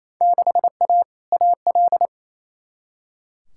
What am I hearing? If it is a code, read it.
Morse code, heard as a keyed tone, about 28 wpm, 700 Hz. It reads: 6A AL
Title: Field Day Exchanges Encore - 28wpm